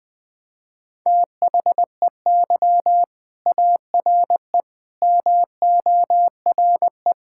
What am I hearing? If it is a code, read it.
Morse code, heard as a keyed tone, about 20 wpm, 700 Hz. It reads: THEY ARE MORE
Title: Sentences from Top 1000 Words - Review Part 5 - 20wpm